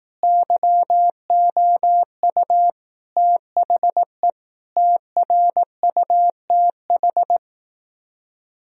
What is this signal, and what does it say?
Morse code, heard as a keyed tone, about 18 wpm, 700 Hz. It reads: YOU THE TRUTH
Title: Speed-Racing - Sets of 5 Words 15wpm